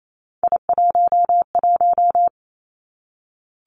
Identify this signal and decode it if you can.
Morse code, heard as a keyed tone, about 28 wpm, 700 Hz. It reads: I11
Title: Three Challenging Letter-Number Combinations 28wpm